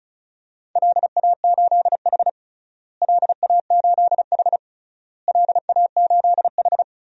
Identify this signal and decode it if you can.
Morse code, heard as a keyed tone, about 34 wpm, 700 Hz. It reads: LA8H LA8H LA8H
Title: DXCC-100 Call Signs - 4 Characters - Repeated 3x - 35wpm